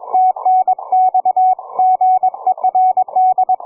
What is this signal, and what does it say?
Morse code, heard as a keyed tone, about 22 wpm, 740 Hz, with radio noise. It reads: TNX GERD
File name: cw forum / v31yn 1826 160m 04h06